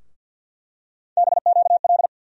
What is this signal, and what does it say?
Morse code, heard as a keyed tone, about 51 wpm, 700 Hz. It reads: BQL
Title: USA Call Sign Postfixes - 3 Letters 50wpm